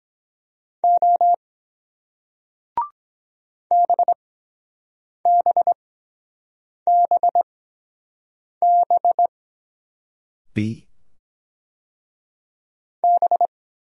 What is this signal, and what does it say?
Morse code, heard as a keyed tone, about 23 wpm, spaced as for 7 wpm, 700 Hz. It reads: O BBBB B